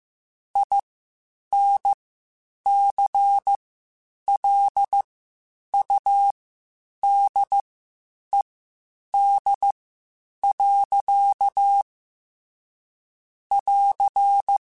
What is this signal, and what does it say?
Morse code, heard as a keyed tone, about 15 wpm, spaced as for 8 wpm, 800 Hz. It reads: INCLUDED. <AR>
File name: Quotes Spaced / Quotes Spaced-mp3s / Airtight sp @13